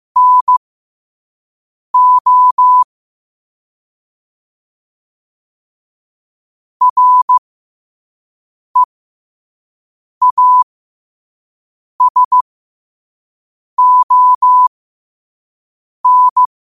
Morse code, heard as a keyed tone, about 15 wpm, spaced as for 5 wpm, 1000 Hz. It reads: NO REASON
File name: qotdfiles / 5wpm